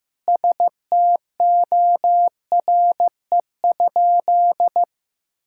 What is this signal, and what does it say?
Morse code, heard as a keyed tone, about 15 wpm, 700 Hz. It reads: STORE?
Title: Sentences from Top 1000 Words - Review Part 5 - 15wpm